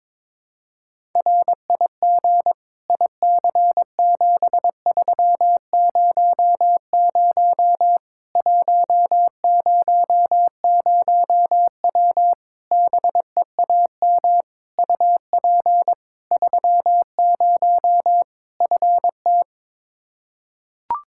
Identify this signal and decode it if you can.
Morse code, heard as a keyed tone, about 22 wpm, 700 Hz. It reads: RIG IC7300 100W BEAM UP 30 FT
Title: Practice QSO - Exchange 2 - Info - 22wpm